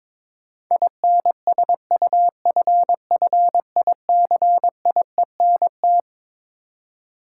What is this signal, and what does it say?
Morse code, heard as a keyed tone, about 21 wpm, 700 Hz. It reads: INSUFFICIENT
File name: Twelve Letter Words - 22wpm